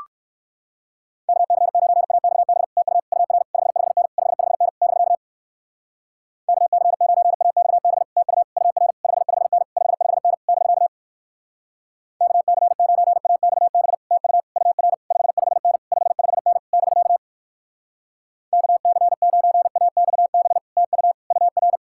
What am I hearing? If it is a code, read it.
Morse code, heard as a keyed tone, about 63 wpm, 700 Hz. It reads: KC9AXB TU UR 55N 55N <BK> KC9AXB TU UR 55N 55N <BK> KC9AXB TU UR 55N 55N <BK> KC9AXB TU UR